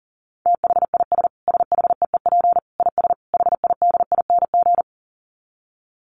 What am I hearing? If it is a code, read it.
Morse code, heard as a keyed tone, about 40 wpm, 700 Hz. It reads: THIS SHEEP IS HIDING
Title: Dit Challenge 40wpm